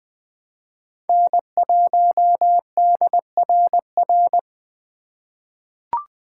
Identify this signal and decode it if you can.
Morse code, heard as a keyed tone, about 20 wpm, 700 Hz. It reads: N1DRR
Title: USA Call Signs - 5 Characters - 2x Character Spacing - 20wpm